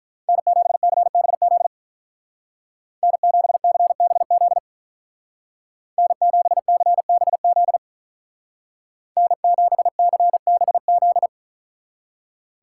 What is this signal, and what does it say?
Morse code, heard as a keyed tone, about 44 wpm, 700 Hz. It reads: N7CBZ N7CBZ N7CBZ N7CBZ
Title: Speed Racing - USA Call Signs 35wpm